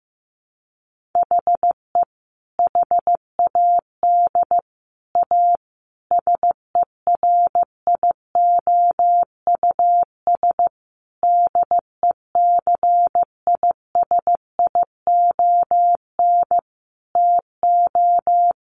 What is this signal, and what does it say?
Morse code, heard as a keyed tone, about 15 wpm, 700 Hz. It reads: HE HAD A SERIOUS DECISION TO
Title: Sentences from Top 1100 Words 15wpm